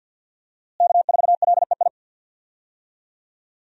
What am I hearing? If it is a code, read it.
Morse code, heard as a keyed tone, about 52 wpm, 700 Hz. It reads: K4LEI
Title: USA Call Signs - 5 Characters - Repeated 3x - 50wpm